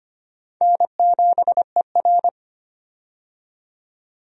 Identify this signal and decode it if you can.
Morse code, heard as a keyed tone, about 25 wpm, 700 Hz. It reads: N7ER
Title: USA Call Signs - 4 Characters - 25wpm